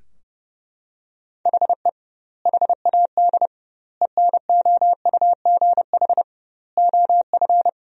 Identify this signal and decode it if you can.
Morse code, heard as a keyed tone, about 32 wpm, 700 Hz. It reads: HE HAD ENOUGH OF
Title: 2x Word-Spacing - Sentences from Top 400 Words - 30wpm